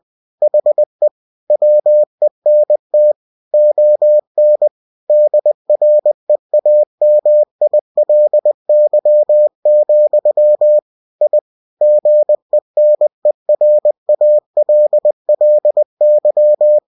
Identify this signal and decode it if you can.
Morse code, heard as a keyed tone, about 20 wpm, 600 Hz. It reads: HE WENT ON DREAMILY, I GENERALLY
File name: cw-books / winnie-the-pooh / winnie-20wpm-page-0030